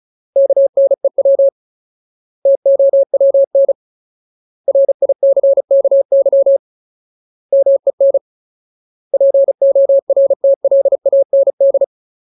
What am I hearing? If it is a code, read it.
Morse code, heard as a keyed tone, about 34 wpm, 550 Hz. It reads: KNEW TOWN RICKY MEN PORTLAND
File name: Ultimate Word List / 35WPM Ultimate